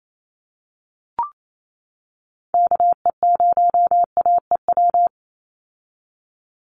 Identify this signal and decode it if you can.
Morse code, heard as a keyed tone, about 28 wpm, 700 Hz. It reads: KE0AEW
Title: USA Call Signs 28wpm